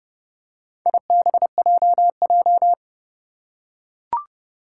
Morse code, heard as a keyed tone, about 29 wpm, 690 Hz. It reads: IBJJ
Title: Four Challenging Letter-Number Combinations 30wpm